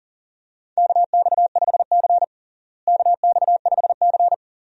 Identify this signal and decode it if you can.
Morse code, heard as a keyed tone, about 40 wpm, 700 Hz. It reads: KX5C KX5C
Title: USA Call Signs - 4 Characters - Repeated 2x - 40wpm